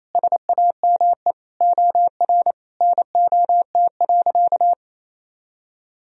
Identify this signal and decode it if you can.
Morse code, heard as a keyed tone, about 28 wpm, 700 Hz. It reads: SAME OR NOT.
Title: Sentences from Top 1000 Words - Review Part 4 - 28wpm